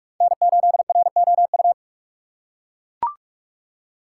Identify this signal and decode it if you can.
Morse code, heard as a keyed tone, about 42 wpm, 700 Hz. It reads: N8ROU